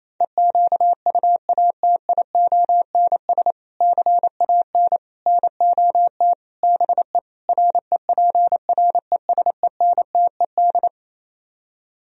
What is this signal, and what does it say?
Morse code, heard as a keyed tone, about 27 wpm, 700 Hz. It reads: EQUATIONS CAN NOT BE REPRESENTED
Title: Speed-Racing - Sets of 5 Words 28wpm